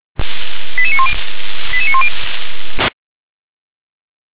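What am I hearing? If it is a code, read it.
Five-tone selective call (ZVEI), heard as quick runs of five tones, tones of about 70 ms. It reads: 88910 88910
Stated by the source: ZVEI five-tone selective call 88148, received by SDR.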